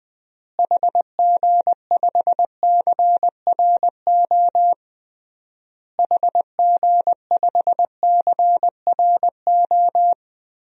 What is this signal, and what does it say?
Morse code, heard as a keyed tone, about 21 wpm, 700 Hz. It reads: HG5CRO HG5CRO
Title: DXCC-100 Call Signs - 6 Characters - Repeated 3x - 20wpm